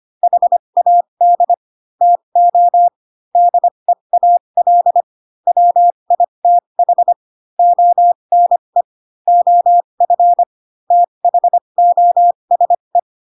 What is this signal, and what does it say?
Morse code, heard as a keyed tone, about 25 wpm, spaced as for 22 wpm, 700 Hz. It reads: HAD TO DEAL WITH ONE OF THOSE
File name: Pinocchio / Pinocchio ch6 18 25 ik1hge